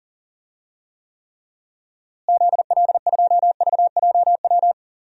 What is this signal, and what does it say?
Morse code, heard as a keyed tone, about 37 wpm, 700 Hz. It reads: ZL2VJW
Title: DXCC-100 Call Signs - 6 Characters - Repeated 3x - 40wpm